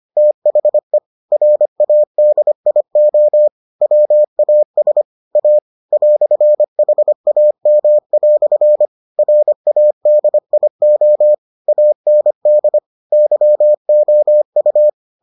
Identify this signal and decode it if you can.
Morse code, heard as a keyed tone, about 25 wpm, 600 Hz. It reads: THE RADIO WAS A "HAM" RADIO AND YOU